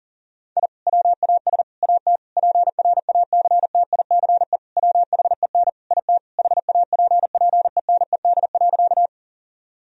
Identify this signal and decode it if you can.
Morse code, heard as a keyed tone, about 40 wpm, 700 Hz. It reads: I WAS AT PRACTICE WHEN IT HAPPENED.
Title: Sentences from Top 1000 Words - Review Part 4 - 40wpm